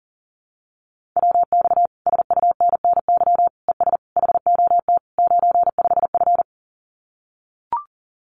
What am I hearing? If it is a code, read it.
Morse code, heard as a keyed tone, about 37 wpm, 700 Hz. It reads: WX SUNNY ES HOT 95F
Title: Common QSO Phrases Encore 40wpm